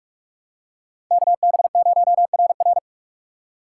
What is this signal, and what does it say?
Morse code, heard as a keyed tone, about 46 wpm, 700 Hz. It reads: KB0RR
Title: USA Call Signs - 5 Characters - 6x Character Spacing - 45wpm